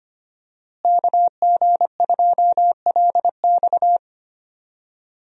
Morse code, heard as a keyed tone, about 25 wpm, 700 Hz. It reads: KG2LX
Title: USA Call Signs - 5 Characters - Repeated 3x - 25wpm